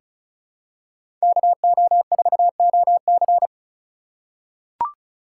Morse code, heard as a keyed tone, about 36 wpm, 700 Hz. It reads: KO4OC
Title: USA Call Signs - 5 Characters - 4x Character Spacing - 35wpm